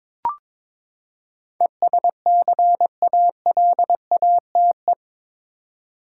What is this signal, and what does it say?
Morse code, heard as a keyed tone, about 22 wpm, 700 Hz. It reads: ESCALATE
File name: practice / 22wpm / Eight Letter Words - 22wpm